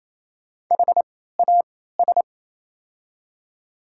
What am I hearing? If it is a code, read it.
Morse code, heard as a keyed tone, about 28 wpm, spaced as for 16 wpm, 700 Hz. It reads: HAS
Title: 3x Character Spacing - Top 100 Words - 28wpm